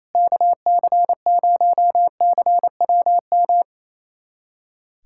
Morse code, heard as a keyed tone, about 28 wpm, 700 Hz. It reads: KC0CWM
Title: USA Callsign DE Callsign 28wpm